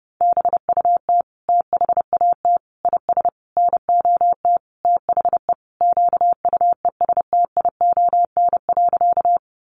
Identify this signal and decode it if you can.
Morse code, heard as a keyed tone, about 30 wpm, 700 Hz. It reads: BUT THAT IS NOT THE QUESTION.
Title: Sentences from Top 600 Words 30wpm